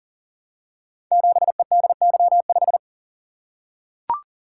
Morse code, heard as a keyed tone, about 38 wpm, 690 Hz. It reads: 7EDY5
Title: Code Groups - 5 Letter:Numbers - 40wpm